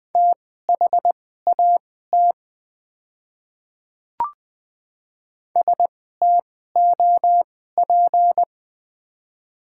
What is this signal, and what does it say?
Morse code, heard as a keyed tone, about 20 wpm, spaced as for 15 wpm, 700 Hz. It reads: THAT STOP
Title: 2x Character Spacing - Top 300 Words - 20wpm